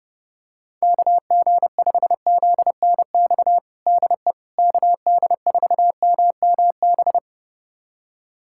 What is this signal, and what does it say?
Morse code, heard as a keyed tone, about 30 wpm, 700 Hz. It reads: KG5ZNX DE KD4MMB
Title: USA Callsign DE Callsign 30wpm